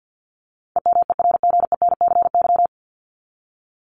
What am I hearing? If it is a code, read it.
Morse code, heard as a keyed tone, about 50 wpm, 700 Hz. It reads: EMERGENCY
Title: Top 1501-1750 Words 50wpm